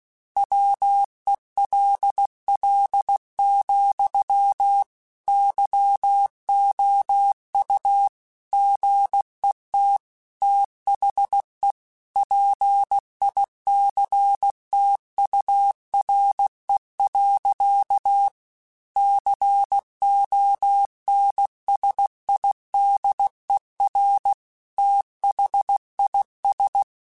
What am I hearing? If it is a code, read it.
Morse code, heard as a keyed tone, about 16 wpm, 800 Hz. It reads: WELL, YOU GET THE PICTURE. CONSIDER THIS
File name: Technical Subjects / Technical Subjects-mp3s / AND Gate @15